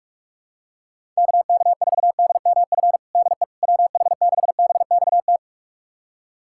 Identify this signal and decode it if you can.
Morse code, heard as a keyed tone, about 42 wpm, 700 Hz. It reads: KK4DMF DE WH6BXT
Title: USA Callsign DE Callsign 45wpm